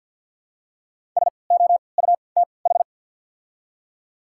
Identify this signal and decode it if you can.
Morse code, heard as a keyed tone, about 50 wpm, spaced as for 28 wpm, 700 Hz. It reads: SOUTH
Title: 3x Character Spacing - Top 500 Words - 50wpm